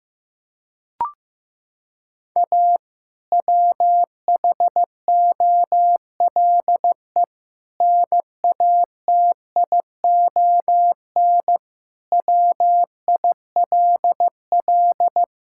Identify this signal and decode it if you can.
Morse code, heard as a keyed tone, about 15 wpm, 700 Hz. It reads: A WHOLE NATION WILL
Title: Sentences from Top 1000 Words - Review Part 6 - 15wpm